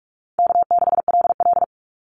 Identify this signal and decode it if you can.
Morse code, heard as a keyed tone, about 44 wpm, 700 Hz. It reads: K6LL
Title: Call Signs - 45wpm